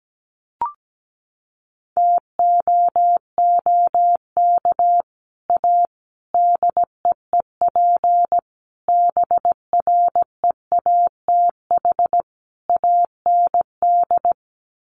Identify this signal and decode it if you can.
Morse code, heard as a keyed tone, about 17 wpm, 700 Hz. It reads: TOOK A DEEP BREATH AND
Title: Sets of 5 Words Encore 17wpm